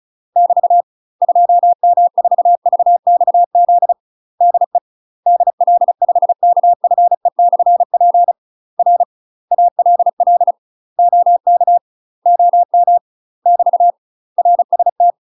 Morse code, heard as a keyed tone, about 36 wpm, 700 Hz. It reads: <BT> 2M4VXZ DE DL5KFE/P R ALL OK OM <BT> RST
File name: UK MASTER QSOs TRAINING / 35WPM / UK MASTER QSOs 0032